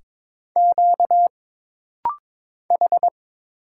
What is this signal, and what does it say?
Morse code, heard as a keyed tone, about 22 wpm, 700 Hz. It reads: Q H
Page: This rapid-fire timing is geared to push your ICR (Instant Character Recognition) skills.